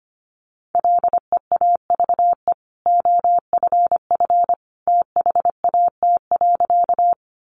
Morse code, heard as a keyed tone, about 25 wpm, 700 Hz. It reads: LEAVE OFF THAT.
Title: Sentences from Top 400 Words 25wpm